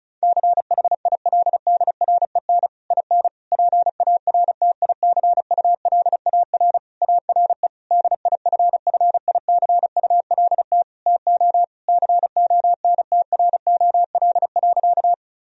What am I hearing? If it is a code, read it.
Morse code, heard as a keyed tone, about 36 wpm, 700 Hz. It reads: CHILDREN IN PARTICULAR ARE DIFFICULT TO CONTROL.
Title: Sentences from Top 1000 Words - Review Part 4 - 35wpm